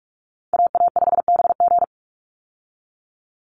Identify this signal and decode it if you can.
Morse code, heard as a keyed tone, about 44 wpm, 700 Hz. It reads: AA5BG
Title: Call Signs Repeated 3x - 45wpm